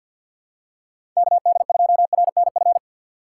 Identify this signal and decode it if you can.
Morse code, heard as a keyed tone, about 51 wpm, 700 Hz. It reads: KD2RNF
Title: USA Call Signs - 4x Character Spacing - 50wpm